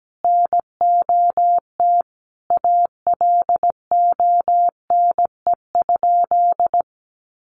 Morse code, heard as a keyed tone, about 17 wpm, 700 Hz. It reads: NOT ALONE?
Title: Sentences from Top 1000 Words 17wpm